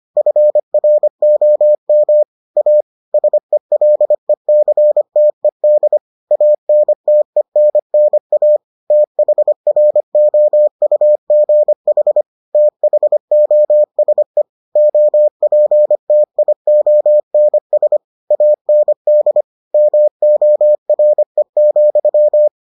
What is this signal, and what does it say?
Morse code, heard as a keyed tone, about 25 wpm, 600 Hz. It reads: FROM A SELECTED ANTENNA THROUGH THOSE OPTIONS AND MORE,